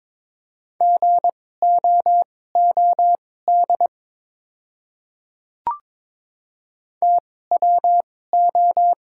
Morse code, heard as a keyed tone, about 22 wpm, spaced as for 16 wpm, 700 Hz. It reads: GOOD TWO
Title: 2x Character Spacing - Top 500 Words - 22wpm